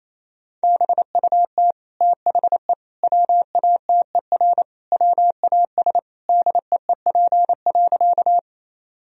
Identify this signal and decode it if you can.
Morse code, heard as a keyed tone, about 28 wpm, 700 Hz. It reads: BUT THE WATER WAS DEEP.
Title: Sentences from Top 1000 Words - Review Part 3 - 28wpm